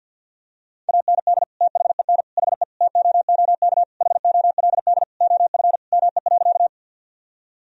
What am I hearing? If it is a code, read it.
Morse code, heard as a keyed tone, about 49 wpm, 700 Hz. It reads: AND THEN HE TOOK HOLD OF ME.
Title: Sentences from Top 1000 Words - Review Part 2 - 50wpm